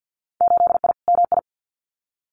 Morse code, heard as a keyed tone, about 50 wpm, 700 Hz. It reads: 8I MI
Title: Winter Field Day Exchanges Encore 50wpm